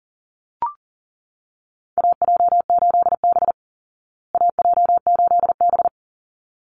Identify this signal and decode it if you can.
Morse code, heard as a keyed tone, about 40 wpm, 700 Hz. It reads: AJ8B AJ8B
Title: Call Signs Repeated 3x - 40wpm